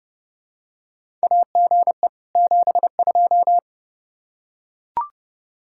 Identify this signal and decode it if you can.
Morse code, heard as a keyed tone, about 30 wpm, 700 Hz. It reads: AGE 72
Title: Practice QSO - Exchange 3 - More Info - 30wpm